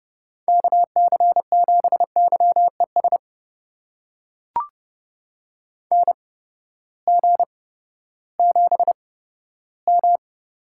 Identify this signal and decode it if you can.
Morse code, heard as a keyed tone, about 30 wpm, spaced as for 15 wpm, 700 Hz. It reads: KC7YES NG7M
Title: USA Call Signs - 8x Character Spacing - 30wpm